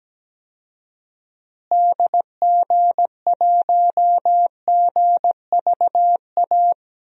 Morse code, heard as a keyed tone, about 17 wpm, 700 Hz. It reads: DG1GVA
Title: DXCC-100 Call Signs - 6 Characters - Repeated 2x - 17wpm